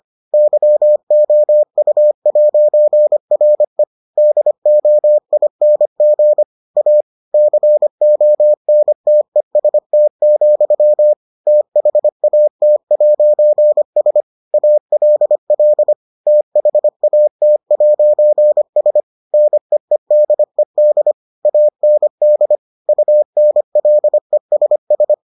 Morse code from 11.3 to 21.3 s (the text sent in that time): THAT'S ALL THAT'S NEEDED